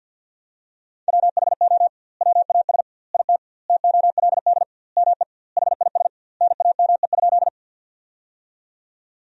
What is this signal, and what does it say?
Morse code, heard as a keyed tone, about 51 wpm, 700 Hz. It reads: WHO WAS IT TOLD ME HIS NAME?
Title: 2x Word-Spacing - Sentences from Top 400 Words - 50wpm